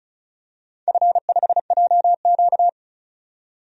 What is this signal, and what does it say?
Morse code, heard as a keyed tone, about 36 wpm, 700 Hz. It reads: F5JQ
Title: DXCC-100 Call Signs - 4 Characters - 35wpm